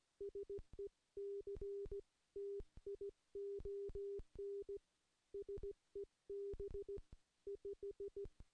Morse code, heard as a keyed tone, about 17 wpm, 400 Hz. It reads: SECTION SEB 5
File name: contests / CW-SS-k5tr- -0700